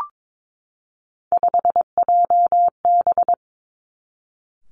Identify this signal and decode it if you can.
Morse code, heard as a keyed tone, about 22 wpm, 700 Hz. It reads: 5JB
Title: Three Challenging Letter-Number Combinations 22wpm